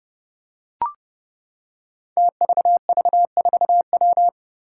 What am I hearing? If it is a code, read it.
Morse code, heard as a keyed tone, about 29 wpm, 700 Hz. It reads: TVV4W
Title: Five Challenging Letter-Number Combinations 30wpm